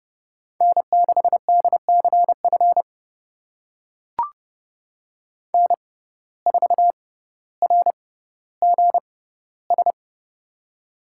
Morse code, heard as a keyed tone, about 30 wpm, spaced as for 15 wpm, 700 Hz. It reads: N6DCF N4RGS
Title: USA Call Signs - 5 Characters - 6x Character Spacing - 30wpm